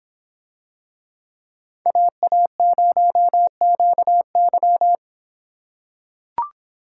Morse code, heard as a keyed tone, about 26 wpm, 700 Hz. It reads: AA0QY